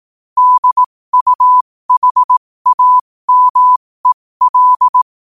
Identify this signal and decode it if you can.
Morse code, heard as a keyed tone, about 18 wpm, spaced as for 15 wpm, 1000 Hz. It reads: DUHAMEL
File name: qotdfiles / 15wpm